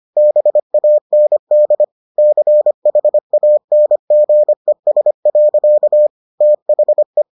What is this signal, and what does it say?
Morse code, heard as a keyed tone, about 25 wpm, 600 Hz. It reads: BAND CHANGES. THE